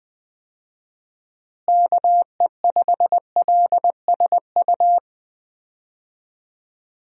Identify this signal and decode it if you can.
Morse code, heard as a keyed tone, about 20 wpm, 700 Hz. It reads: KE5LSU
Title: USA Call Signs - 6 Characters - Repeated 2x - 20wpm